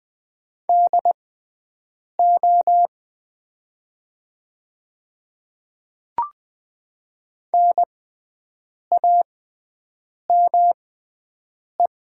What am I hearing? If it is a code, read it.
Morse code, heard as a keyed tone, about 20 wpm, spaced as for 7 wpm, 700 Hz. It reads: DO NAME
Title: 6x Character Spacing - Top 200 Words - 20wpm